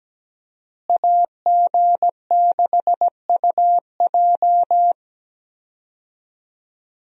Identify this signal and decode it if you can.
Morse code, heard as a keyed tone, about 17 wpm, 700 Hz. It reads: AG6UJ
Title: USA Call Signs - 5 Characters - Repeated 2x - 17wpm